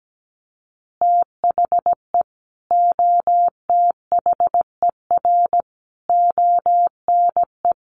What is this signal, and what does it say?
Morse code, heard as a keyed tone, about 17 wpm, 700 Hz. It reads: THE OTHER ONE
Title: Additional Easy Phrases 17wpm